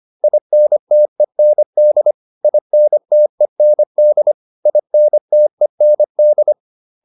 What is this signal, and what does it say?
Morse code, heard as a keyed tone, about 25 wpm, 600 Hz. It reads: INTEND INTEND INTEND